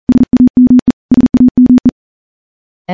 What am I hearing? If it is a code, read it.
Morse code, heard as a keyed tone, about 38 wpm, 250 Hz. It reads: SAME SAME
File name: morse / morse-training-40WPM